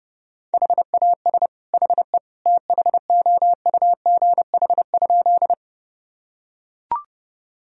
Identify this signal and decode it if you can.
Morse code, heard as a keyed tone, about 30 wpm, 700 Hz. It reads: HAS HE THOUGH?
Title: Sentences from Top 1000 Words - Review Part 2 - 30wpm